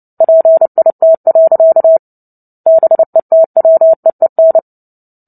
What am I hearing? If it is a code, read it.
Morse code, heard as a keyed tone, about 30 wpm, 650 Hz. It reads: PIT. BETWEEN
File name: Book Two / 03-Chapter 3 - 29 WPM